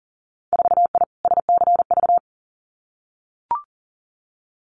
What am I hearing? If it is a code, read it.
Morse code, heard as a keyed tone, about 38 wpm, 700 Hz. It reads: 4I SCV
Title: Winter Field Day Exchanges Encore 40wpm